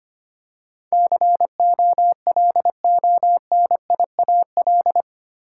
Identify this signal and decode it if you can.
Morse code, heard as a keyed tone, about 25 wpm, 700 Hz. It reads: COLONIAL
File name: Eight Letter Words - 25wpm